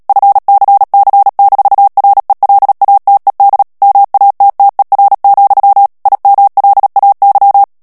Morse code, heard as a keyed tone, about 37 wpm, 800 Hz. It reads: FCC-RELATED MATTER, IMLAY
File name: morse mp3 / ANGLAIS texte clair 3 vitesse 35 mots minute